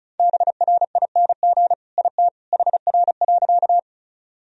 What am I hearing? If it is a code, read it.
Morse code, heard as a keyed tone, about 34 wpm, 700 Hz. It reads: BRING IT HR.
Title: Sentences from Top 500 Words and Abbreviations 35wpm